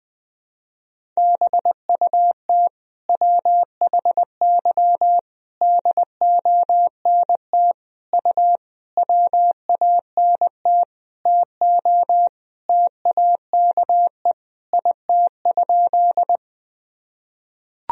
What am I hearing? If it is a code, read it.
Morse code, heard as a keyed tone, about 20 wpm, 700 Hz. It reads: BUT WHY DONT U WANT TO TAKE IT?
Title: Sentences from Top 300 Words and Abbreviations 20wpm